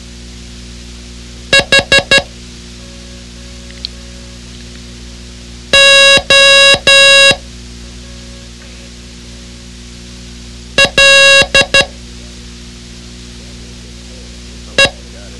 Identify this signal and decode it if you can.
Morse code, heard as a keyed tone, about 12 wpm, spaced as for 2 wpm, 2750 Hz. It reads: HOLE